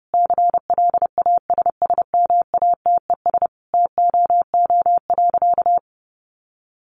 Morse code, heard as a keyed tone, about 28 wpm, 700 Hz. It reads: CLASSMATES TOO.
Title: Additional Easy Phrases 30wpm